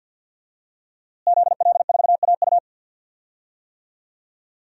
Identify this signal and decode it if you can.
Morse code, heard as a keyed tone, about 51 wpm, 700 Hz. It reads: ZL4AU
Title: DXCC-100 Call Signs - 5 Characters - Repeated 2x - 50wpm